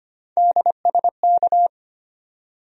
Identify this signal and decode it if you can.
Morse code, heard as a keyed tone, about 25 wpm, 700 Hz. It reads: DSK
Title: USA Call Sign Postfixes - 3 Letters 25wpm